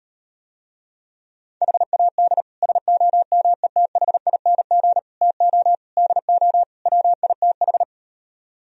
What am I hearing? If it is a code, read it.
Morse code, heard as a keyed tone, about 38 wpm, 700 Hz. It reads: HAD SOMETHING TO DO WITH